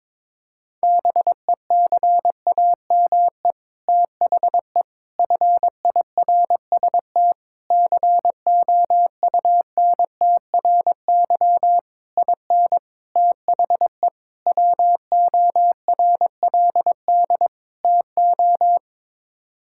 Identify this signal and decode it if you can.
Morse code, heard as a keyed tone, about 22 wpm, 700 Hz. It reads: BECAME THE FIRST COUNTRY IN THE WORLD TO
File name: practice / 22wpm / Sets of 8 Words - 22wpm